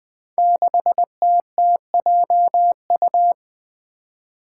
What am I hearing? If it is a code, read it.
Morse code, heard as a keyed tone, about 20 wpm, 700 Hz. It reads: 6TTJU
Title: Five Challenging Letter-Number Combinations 20wpm